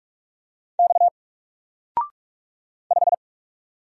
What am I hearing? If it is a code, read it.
Morse code, heard as a keyed tone, about 44 wpm, 700 Hz. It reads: X 5
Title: Single Letter-Number Alt 45wpm